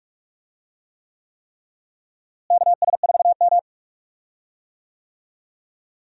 Morse code, heard as a keyed tone, about 46 wpm, 700 Hz. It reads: KS4M
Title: USA Call Signs - 4 Characters - Repeated 3x - 45wpm